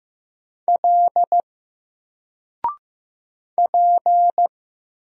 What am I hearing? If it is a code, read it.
Morse code, heard as a keyed tone, about 15 wpm, 700 Hz. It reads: L P
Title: Single Letter-Number 15wpm